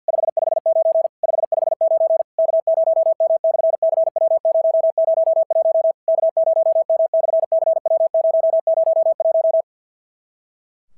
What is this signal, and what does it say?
Morse code, heard as a keyed tone, about 54 wpm, 650 Hz. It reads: 559 559 K0M/CW001 K0M/CW001
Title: SOTA QSO Practice 50wpm